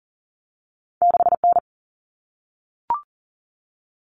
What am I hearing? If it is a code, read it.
Morse code, heard as a keyed tone, about 38 wpm, 700 Hz. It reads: 6N
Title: All Cut Numbers - 2-Digit 40wpm